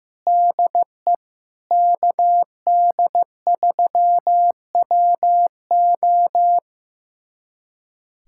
Morse code, heard as a keyed tone, about 15 wpm, 700 Hz. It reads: DE KD3WO
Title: USA Callsign DE Callsign 15wpm